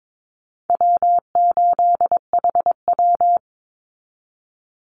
Morse code, heard as a keyed tone, about 22 wpm, 700 Hz. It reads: W8HW
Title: Call Signs - 22wpm